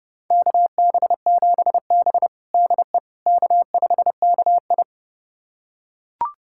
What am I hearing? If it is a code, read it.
Morse code, heard as a keyed tone, about 30 wpm, 700 Hz. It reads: KB7B DE K5KI
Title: USA Callsign DE Callsign - 4 Characters - 30wpm